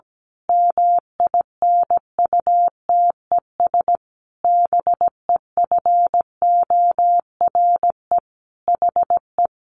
Morse code, heard as a keyed tone, about 17 wpm, 700 Hz. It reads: MINUTES BEFORE HE
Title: A Sound Of Thunder 17wpm